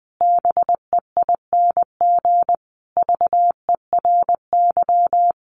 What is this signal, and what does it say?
Morse code, heard as a keyed tone, about 20 wpm, 700 Hz. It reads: BEING VERY
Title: Sentences from Top 1500 Words 20wpm